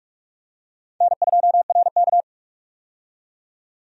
Morse code, heard as a keyed tone, about 43 wpm, 700 Hz. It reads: N2RK
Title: USA Call Signs - 4 Characters - 45wpm